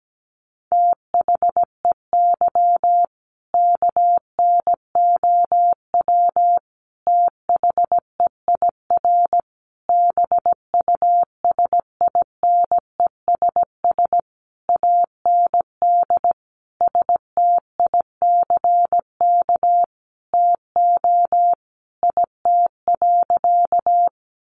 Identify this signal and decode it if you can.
Morse code, heard as a keyed tone, about 17 wpm, 700 Hz. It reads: THEY KNOW THEIR BUSINESS AND STICK TO IT.
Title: Sentences from Top 1200 Words 17wpm